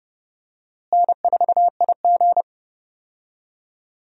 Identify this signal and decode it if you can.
Morse code, heard as a keyed tone, about 32 wpm, 700 Hz. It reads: N4IG
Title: USA Call Signs - 4 Characters - Repeated 2x - 30wpm